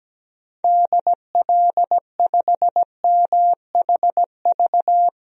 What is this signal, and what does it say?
Morse code, heard as a keyed tone, about 17 wpm, 700 Hz. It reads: DL5MHV
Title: DXCC-100 Call Signs - 6 Characters - Repeated 2x - 17wpm